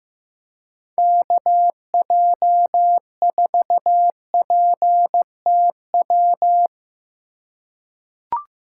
Morse code, heard as a keyed tone, about 15 wpm, 700 Hz. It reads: KJ4PTW
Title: USA Call Signs - 6 Characters - 4x Character Spacing - 15wpm